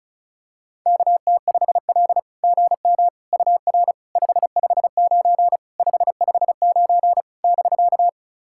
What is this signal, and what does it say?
Morse code, heard as a keyed tone, about 36 wpm, 690 Hz. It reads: KT5L GM UR 559 559 <BK>
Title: SOTA QSO Practice 35wpm